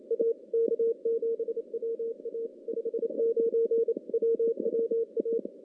Morse code, heard as a keyed tone, about 28 wpm, 460 Hz. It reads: A K7WA 59 WWA
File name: contests / CW-SS-k5tr- -2300